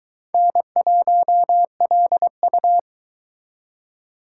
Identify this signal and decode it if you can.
Morse code, heard as a keyed tone, about 23 wpm, 700 Hz. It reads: N1LU